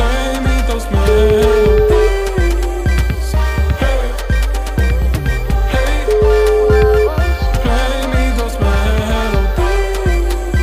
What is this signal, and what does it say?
Ringback tone (Europe, 425 Hz), with noise.